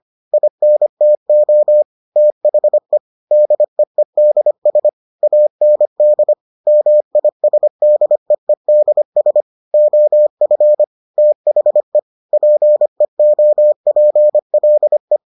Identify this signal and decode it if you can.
Morse code, heard as a keyed tone, about 25 wpm, 600 Hz. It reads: INTO THE DEEDS AND MISDEEDS OF THE PEOPLE